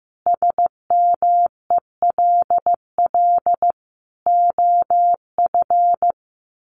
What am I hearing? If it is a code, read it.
Morse code, heard as a keyed tone, about 15 wpm, 700 Hz. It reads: SMELL OF
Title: A Sound Of Thunder 15wpm